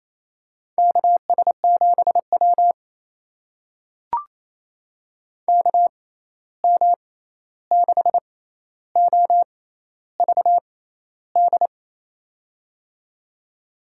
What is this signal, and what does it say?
Morse code, heard as a keyed tone, about 27 wpm, spaced as for 13 wpm, 700 Hz. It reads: KS7W KM6OVD
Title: USA Call Signs - 6x Character Spacing - 28wpm